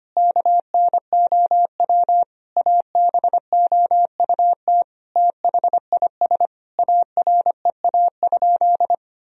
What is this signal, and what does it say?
Morse code, heard as a keyed tone, about 25 wpm, 700 Hz. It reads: KNOW ABOUT THIS AREA?
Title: Sentences from Top 1000 Words - Review Part 2 - 25wpm